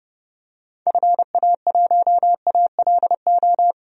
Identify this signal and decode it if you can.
Morse code, heard as a keyed tone, about 32 wpm, 700 Hz. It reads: FA1ALO
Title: DXCC-100 Call Signs - 6 Characters - Repeated 2x - 30wpm